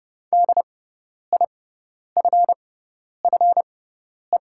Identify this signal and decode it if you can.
Morse code, heard as a keyed tone, about 30 wpm, spaced as for 10 wpm, 700 Hz. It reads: DIFFE
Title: 6x Character Spacing - Top 300 Words - 30wpm